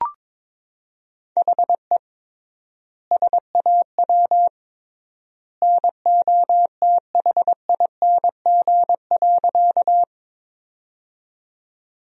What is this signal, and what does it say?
Morse code, heard as a keyed tone, about 22 wpm, 700 Hz. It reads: HE SAW NOTHING.
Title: 3x Word-Spacing - Sentences from Top 500 Words - 22wpm